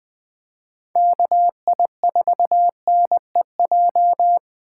Morse code, heard as a keyed tone, about 20 wpm, 700 Hz. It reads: KI4NEJ
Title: Speed Racing - USA Call Signs 17wpm